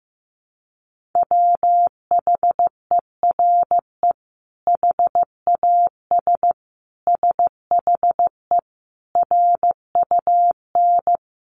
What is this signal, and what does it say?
Morse code, heard as a keyed tone, about 15 wpm, 700 Hz. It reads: WHERE HAS SHE RUN
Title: Sentences from Top 300 Words 15wpm